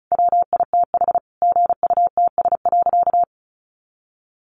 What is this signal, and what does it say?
Morse code, heard as a keyed tone, about 34 wpm, 700 Hz. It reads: WITH GUTS.
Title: A Sound Of Thunder 35wpm